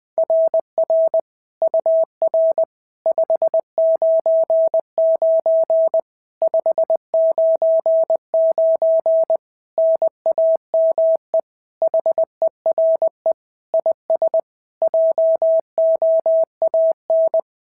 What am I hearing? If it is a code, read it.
Morse code, heard as a keyed tone, about 21 wpm, 690 Hz. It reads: RR UR 599 599 NAME HERE IS JOAN
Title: SOTA QSO Practice 20wpm